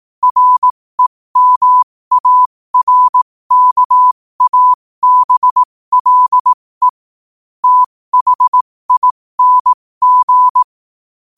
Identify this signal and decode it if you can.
Morse code, heard as a keyed tone, about 18 wpm, spaced as for 15 wpm, 1000 Hz. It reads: REMARKABLE THING
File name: qotdfiles / 15wpm